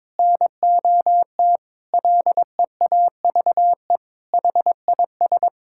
Morse code, heard as a keyed tone, about 22 wpm, 700 Hz. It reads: NOT LEAVE HIS
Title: Sentences from Top 1000 Words - Review Part 6 - 22wpm